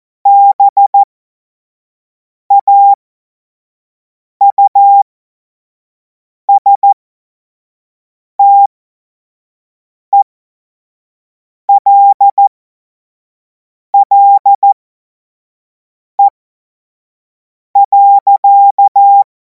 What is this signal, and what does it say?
Morse code, heard as a keyed tone, about 14 wpm, spaced as for 5 wpm, 800 Hz. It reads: BAUSTELLE.
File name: Morsekurs / Witze / Witz 03. Februar 2026 25 F70